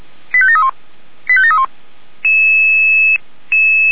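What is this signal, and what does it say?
Five-tone selective call (ZVEI), heard as quick runs of five tones, tones of about 70 ms. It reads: 85731 85731
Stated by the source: ZVEI five-tone selective call, address 89145, received on an SDR.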